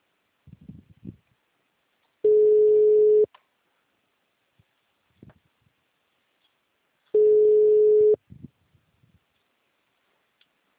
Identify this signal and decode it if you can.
Ringback tone (Europe, 425 Hz).
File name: out- -2005- - - .16046